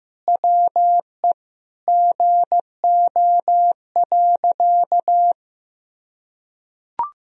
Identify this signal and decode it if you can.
Morse code, heard as a keyed tone, about 15 wpm, 700 Hz. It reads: WE GO.
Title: Sentences from Top 1000 Words - Review Part 3 - 15wpm